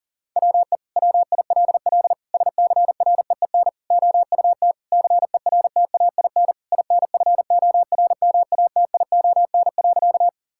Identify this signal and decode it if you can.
Morse code, heard as a keyed tone, about 40 wpm, 690 Hz. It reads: WE WILL SCREEN OUT CERTAIN INFORMATION.
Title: Sentences from Top 1000 Words - Review Part 6 - 40wpm